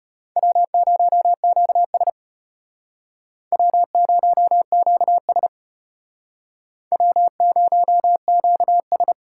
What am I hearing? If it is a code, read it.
Morse code, heard as a keyed tone, about 32 wpm, 700 Hz. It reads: W0QS W0QS W0QS